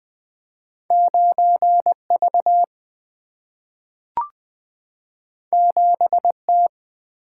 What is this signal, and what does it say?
Morse code, heard as a keyed tone, about 20 wpm, 690 Hz. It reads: 9V 7T
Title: Two Letter-Number Combinations - 20wpm